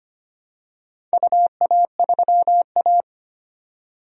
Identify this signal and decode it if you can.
Morse code, heard as a keyed tone, about 25 wpm, 700 Hz. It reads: UA3A
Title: DXCC-100 Call Signs - 4 Characters - 25wpm